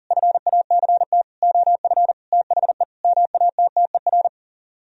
Morse code, heard as a keyed tone, about 52 wpm, 700 Hz. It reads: FACT OF THE MATTER